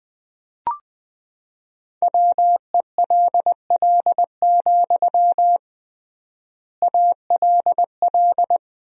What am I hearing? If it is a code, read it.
Morse code, heard as a keyed tone, about 21 wpm, 700 Hz. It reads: WELL, ALL
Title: 3x Word-Spacing - Sentences from Top 200 Words - 20wpm